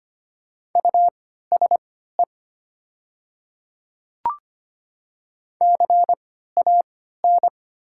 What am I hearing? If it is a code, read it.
Morse code, heard as a keyed tone, about 25 wpm, spaced as for 14 wpm, 700 Hz. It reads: USE CAN
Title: 3x Character Spacing - Top 200 Words - 25wpm